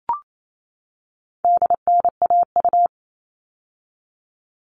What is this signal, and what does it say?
Morse code, heard as a keyed tone, about 28 wpm, 700 Hz. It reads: DNAU
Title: All Cut Numbers - 4-Digit 28wpm